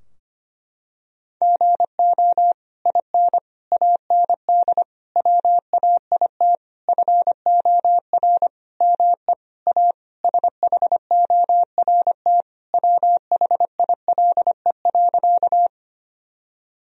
Morse code, heard as a keyed tone, about 25 wpm, 700 Hz. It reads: GO IN AND WAIT FOR ME A SHORT WHILE.
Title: Sentences from Top 1000 Words - Review Part 2 - 25wpm